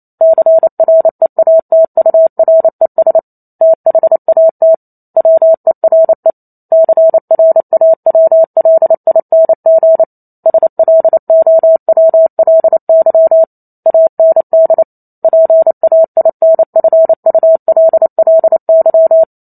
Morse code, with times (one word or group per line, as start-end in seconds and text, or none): 0.2-3.2 CREATURES
3.6-4.7 THAT
5.2-6.3 WERE
6.7-10.0 CRAWLING
10.5-13.4 SLOWLY
13.9-14.8 AND
15.2-19.2 PAINFULLY